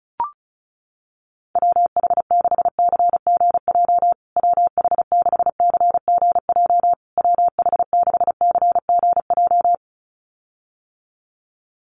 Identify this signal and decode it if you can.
Morse code, heard as a keyed tone, about 34 wpm, 700 Hz. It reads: WH6CGJ WH6CGJ WH6CGJ
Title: USA Call Signs Repeated 3x 35wpm